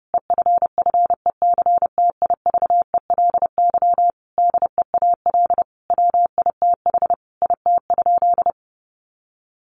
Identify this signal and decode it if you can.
Morse code, heard as a keyed tone, about 28 wpm, 700 Hz. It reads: EFFECTIVELY DEAL WITH IT?
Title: Sentences from Top 1400 Words 30wpm